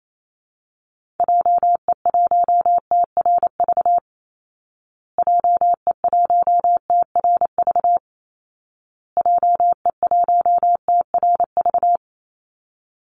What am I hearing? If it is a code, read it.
Morse code, heard as a keyed tone, about 28 wpm, 700 Hz. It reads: JE1TRV JE1TRV JE1TRV
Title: Call Signs Repeated 2x - 28wpm